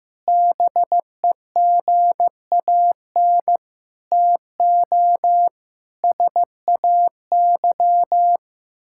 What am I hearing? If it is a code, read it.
Morse code, heard as a keyed tone, about 15 wpm, 700 Hz. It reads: BEGAN TO SAY
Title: Sentences from Top 1000 Words - Review Part 5 - 15wpm